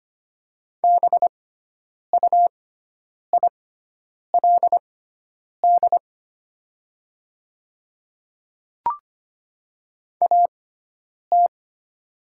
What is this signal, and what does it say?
Morse code, heard as a keyed tone, about 25 wpm, spaced as for 9 wpm, 700 Hz. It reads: BUILD AT
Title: 6x Character Spacing - Top 200 Words - 25wpm